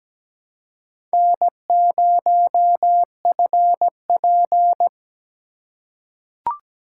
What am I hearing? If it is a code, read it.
Morse code, heard as a keyed tone, about 17 wpm, 700 Hz. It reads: N0FP
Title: USA Call Signs - 4 Characters - 4x Character Spacing - 17wpm